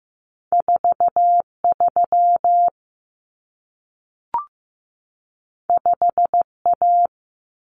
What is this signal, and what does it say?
Morse code, heard as a keyed tone, about 15 wpm, 700 Hz. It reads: 43 5A
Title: Common Cut Numbers - 2-Digit 15wpm